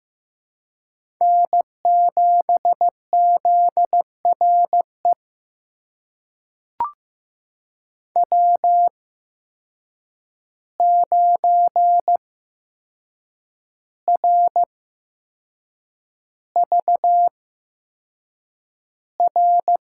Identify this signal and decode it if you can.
Morse code, heard as a keyed tone, about 15 wpm, spaced as for 6 wpm, 700 Hz. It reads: N7ZRE W9RVR
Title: USA Call Signs - 5 Characters - 8x Character Spacing - 15wpm